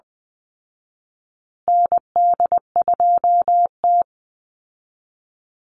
Morse code, heard as a keyed tone, about 20 wpm, 700 Hz. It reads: ND2T
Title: Call Signs Repeated 2x - 20wpm